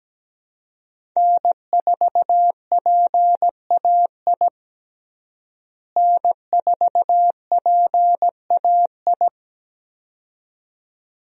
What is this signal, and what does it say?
Morse code, heard as a keyed tone, about 17 wpm, 700 Hz. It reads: N4PAI N4PAI
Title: USA Call Signs - 5 Characters - Repeated 2x - 17wpm